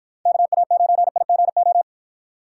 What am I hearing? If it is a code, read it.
Morse code, heard as a keyed tone, about 50 wpm, 700 Hz. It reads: KA9IGO